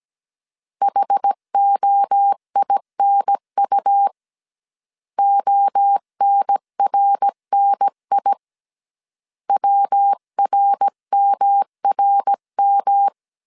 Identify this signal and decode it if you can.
Morse code, heard as a keyed tone, about 17 wpm, 800 Hz. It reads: HOINU ONRNI WRMRM
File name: CW / DET / 1 3 1500C S A 16wpm 2141